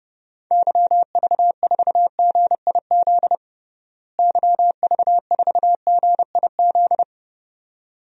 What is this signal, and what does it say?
Morse code, heard as a keyed tone, about 30 wpm, 700 Hz. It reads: YV4GIZ YV4GIZ
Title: DXCC-100 Call Signs - 6 Characters - Repeated 2x - 30wpm